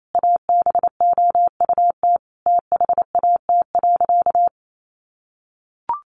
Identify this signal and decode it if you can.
Morse code, heard as a keyed tone, about 28 wpm, 700 Hz. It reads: ABOUT THAT.
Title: Sentences from Top 1200 Words 28wpm